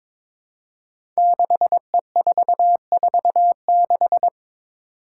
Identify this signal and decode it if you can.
Morse code, heard as a keyed tone, about 22 wpm, 700 Hz. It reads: 6E446
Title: Five Challenging Letter-Number Combinations 22wpm